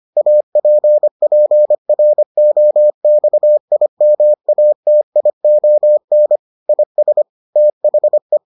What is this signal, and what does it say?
Morse code, heard as a keyed tone, about 25 wpm, 600 Hz. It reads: APPROXIMATION IS THE